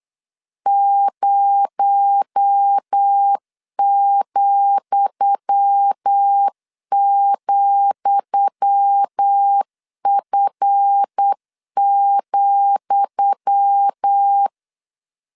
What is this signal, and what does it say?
Morse code, heard as a keyed tone, about 8 wpm, 800 Hz. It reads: 0,,F,
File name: LEZIONI / 10 SIGNS / 10D COMA 8wpm 3749